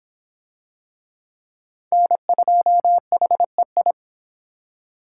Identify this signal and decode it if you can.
Morse code, heard as a keyed tone, about 23 wpm, 700 Hz. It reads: N2HEI